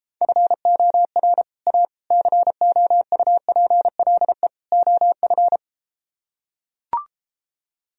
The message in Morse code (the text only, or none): FOR A COUPLE OF